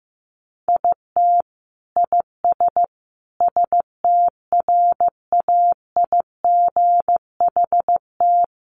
Morse code, heard as a keyed tone, about 15 wpm, 700 Hz. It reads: IT IS STRAIGHT
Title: Sentences from Top 1000 Words 15wpm